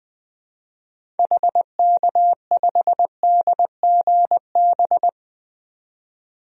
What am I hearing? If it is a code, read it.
Morse code, heard as a keyed tone, about 21 wpm, 700 Hz. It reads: HK5DGB
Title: DXCC-100 Call Signs - 6 Characters - 20wpm